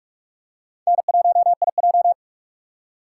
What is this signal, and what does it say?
Morse code, heard as a keyed tone, about 45 wpm, 700 Hz. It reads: N1IJ
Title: USA Call Signs - 4 Characters - 4x Character Spacing - 45wpm